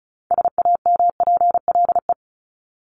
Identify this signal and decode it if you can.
Morse code, heard as a keyed tone, about 34 wpm, 700 Hz. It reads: SAMPLE
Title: Top 1501-1750 Words 35wpm